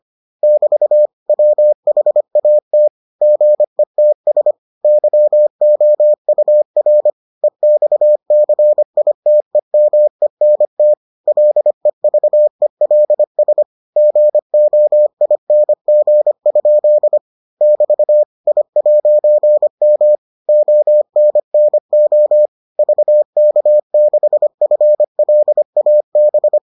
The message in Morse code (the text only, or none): <BT> WHAT GETS YOUR EXCITEMENT LEVELS GOING? <BT> I'M ONNO VK6FLAB